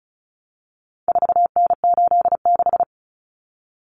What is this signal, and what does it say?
Morse code, heard as a keyed tone, about 34 wpm, 700 Hz. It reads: 4N86
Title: Common Cut Numbers - 4-Digit 35wpm